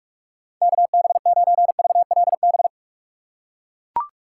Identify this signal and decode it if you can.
Morse code, heard as a keyed tone, about 44 wpm, 700 Hz. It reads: KB9VLB
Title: USA Call Signs - 6 Characters - 6x Character Spacing - 45wpm